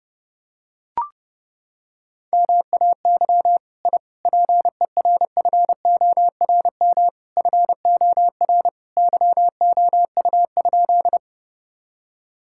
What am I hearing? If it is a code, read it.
Morse code, heard as a keyed tone, about 30 wpm, 700 Hz. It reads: MAY I PERFORM FOR YOU?
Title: Sentences from Top 1000 Words - Review Part 5 - 30wpm